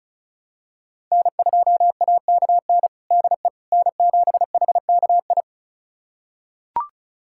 Morse code, heard as a keyed tone, about 36 wpm, 700 Hz. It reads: N2AKN DE N7HKI
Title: USA Callsign DE Callsign - 5 Characters - 35wpm